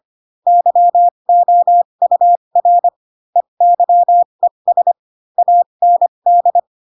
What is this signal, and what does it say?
Morse code, heard as a keyed tone, about 25 wpm, spaced as for 22 wpm, 700 Hz. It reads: YOUR EYES AND